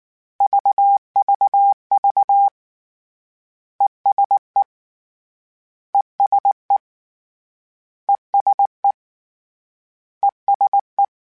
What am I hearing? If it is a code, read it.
Morse code, heard as a keyed tone, about 19 wpm, 800 Hz. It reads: VVV ESE ESE ESE ESE
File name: Short Runs / Short Runs-mp3s / ESE ING @18